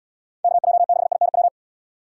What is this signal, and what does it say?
Morse code, heard as a keyed tone, about 75 wpm, 700 Hz. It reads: KJ6INW